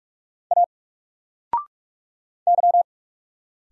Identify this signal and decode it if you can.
Morse code, heard as a keyed tone, about 46 wpm, 700 Hz. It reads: A Y
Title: Single Letter-Number 45wpm